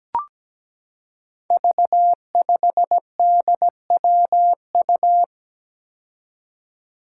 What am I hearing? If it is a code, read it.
Morse code, heard as a keyed tone, about 17 wpm, 700 Hz. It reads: V5DWU
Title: Five Challenging Letter-Number Combinations 17wpm